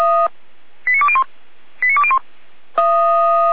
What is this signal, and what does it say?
Five-tone selective call (ZVEI), heard as quick runs of five tones, tones of about 70 ms. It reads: 89201 89201